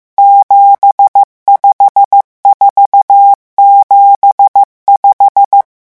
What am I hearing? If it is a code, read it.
Morse code, heard as a keyed tone, about 15 wpm, 800 Hz. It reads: 75475
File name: morse mp3 / revision11 chiffres vitesse 14